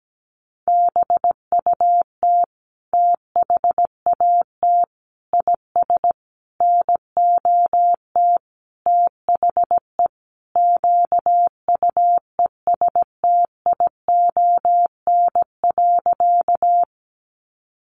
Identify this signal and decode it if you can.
Morse code, heard as a keyed tone, about 17 wpm, 700 Hz. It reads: BUT THAT IS NOT THE QUESTION.
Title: Sentences from Top 600 Words 17wpm